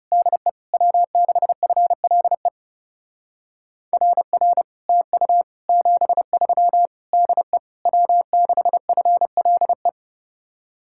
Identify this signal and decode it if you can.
Morse code, heard as a keyed tone, about 36 wpm, 700 Hz. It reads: DE W6FLE RR TU 73 DE W6FLE
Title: Speed-Racing Common QSO Phrases - 30wpm